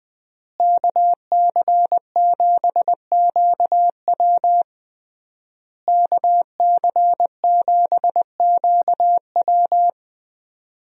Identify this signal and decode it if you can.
Morse code, heard as a keyed tone, about 21 wpm, 700 Hz. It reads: KC7QW KC7QW
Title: USA Call Signs - 5 Characters - Repeated 3x - 20wpm